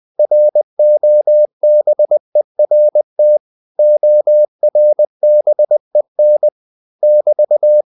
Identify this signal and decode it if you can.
Morse code, heard as a keyed tone, about 20 wpm, 600 Hz. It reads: ROBERT ORBEN <BT>
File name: CW / Practice / EnglishQuotations0320